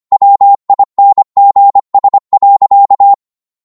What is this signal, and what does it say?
Morse code, heard as a keyed tone, about 25 wpm, 800 Hz. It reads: WINGS.
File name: Morse / psg-25wpm